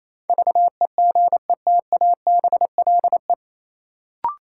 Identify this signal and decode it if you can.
Morse code, heard as a keyed tone, about 28 wpm, 700 Hz. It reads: VEGETABLE
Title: Sentences from Top 1300 Words - New Word Drill - 28wpm